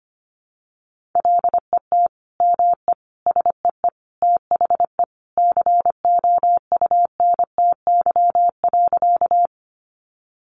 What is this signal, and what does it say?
Morse code, heard as a keyed tone, about 25 wpm, 700 Hz. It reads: LET ME SEE THE COUNTY.
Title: Sentences from Top 1200 Words 25wpm